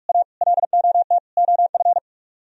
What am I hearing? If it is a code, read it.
Morse code, heard as a keyed tone, about 61 wpm, spaced as for 53 wpm, 700 Hz. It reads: A LOT OF
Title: Speed-Racing - Sets of 4 Words 45wpm